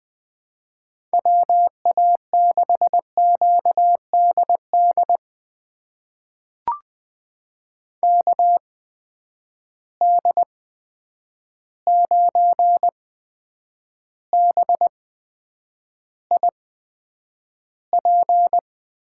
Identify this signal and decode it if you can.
Morse code, heard as a keyed tone, about 20 wpm, spaced as for 9 wpm, 700 Hz. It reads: WA6QDD KD9BIP
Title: USA Call Signs - 6 Characters - 8x Character Spacing - 20wpm